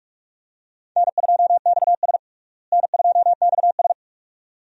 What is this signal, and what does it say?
Morse code, heard as a keyed tone, about 42 wpm, 700 Hz. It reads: N2XS N2XS
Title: USA Call Signs - 4 Characters - Repeated 2x - 45wpm